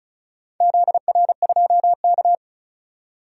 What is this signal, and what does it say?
Morse code, heard as a keyed tone, about 34 wpm, 700 Hz. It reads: ZR2K
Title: DXCC-100 Call Signs - 4 Characters - 35wpm